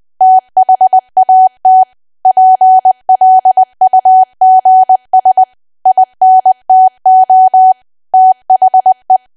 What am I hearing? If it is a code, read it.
Morse code, heard as a keyed tone, about 20 wpm, 750 Hz. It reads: THAT PLUGS INTO THE
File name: cwa / W1AW-Level3 / 20WPM